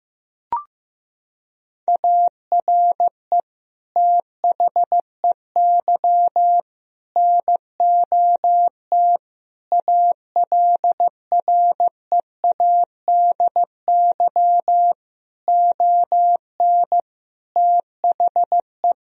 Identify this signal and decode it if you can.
Morse code, heard as a keyed tone, about 15 wpm, 700 Hz. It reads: ARE THEY NOT ALREADY ON THE
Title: Sentences from Top 1000 Words - Review Part 1 - 15wpm